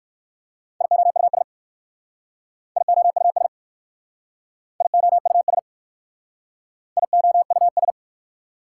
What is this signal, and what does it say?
Morse code, heard as a keyed tone, about 58 wpm, 690 Hz. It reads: IOUS IOUS IOUS IOUS